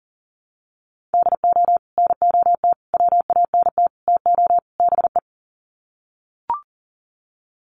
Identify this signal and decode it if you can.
Morse code, heard as a keyed tone, about 40 wpm, 700 Hz. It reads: DO NOT WANT TO BE
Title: Sets of 5 Words 40wpm